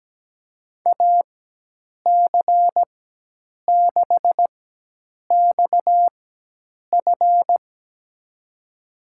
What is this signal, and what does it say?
Morse code, heard as a keyed tone, about 17 wpm, spaced as for 8 wpm, 700 Hz. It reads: AC6XF
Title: USA Call Signs - 4x Character Spacing - 17wpm